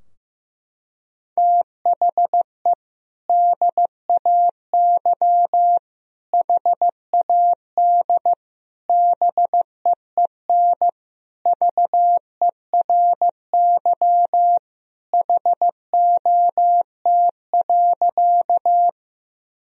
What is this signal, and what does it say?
Morse code, heard as a keyed tone, about 15 wpm, 700 Hz. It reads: THE DAY HAD BEEN VERY HOT.
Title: Sentences from Top 1000 Words - Review Part 3 - 15wpm